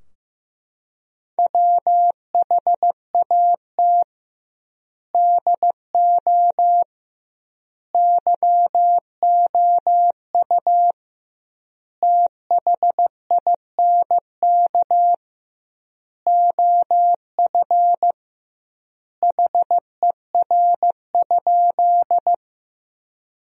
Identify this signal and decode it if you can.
Morse code, heard as a keyed tone, about 15 wpm, 700 Hz. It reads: WHAT DO YOU THINK OF HER?
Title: 2x Word-Spacing - Sentences from Top 200 Words - 15wpm